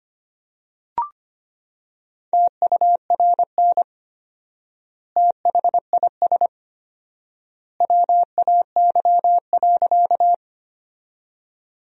Morse code, heard as a keyed tone, about 25 wpm, 700 Hz. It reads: TURN THIS WAY.
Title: 4x Word-Spacing - Sentences from Top 200 Words - 25wpm